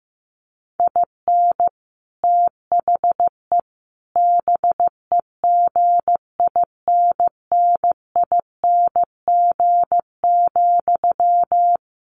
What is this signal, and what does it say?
Morse code, heard as a keyed tone, about 15 wpm, 700 Hz. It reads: IN THE BEGINNING,
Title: Sentences from Top 1400 Words 15wpm